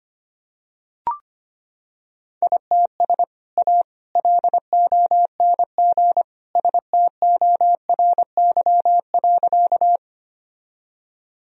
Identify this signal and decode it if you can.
Morse code, heard as a keyed tone, about 25 wpm, 700 Hz. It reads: ITS A LONG STORY.
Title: Sentences from Top 1000 Words - Review Part 1 - 25wpm